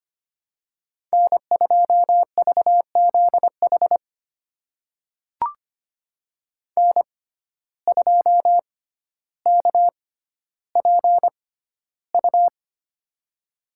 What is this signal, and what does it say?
Morse code, heard as a keyed tone, about 25 wpm, spaced as for 13 wpm, 700 Hz. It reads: N2VZH N2KPU
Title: USA Call Signs - 6x Character Spacing - 25wpm